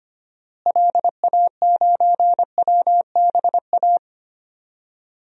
Morse code, heard as a keyed tone, about 25 wpm, 700 Hz. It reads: LA9WBA
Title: DXCC-100 Call Signs - 6 Characters - Repeated 3x - 25wpm